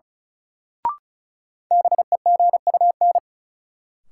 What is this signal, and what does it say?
Morse code, heard as a keyed tone, about 36 wpm, 700 Hz. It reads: BEGUN
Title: Sentences from Top 1000 Words - Review Part 6 - New Word Drill - 35wpm